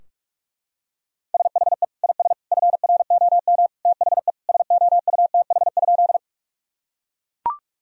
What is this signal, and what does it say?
Morse code, heard as a keyed tone, about 42 wpm, 700 Hz. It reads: SHE IS FROM THE SOUTH?
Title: Sentences from Top 1000 Words - Review Part 2 - 45wpm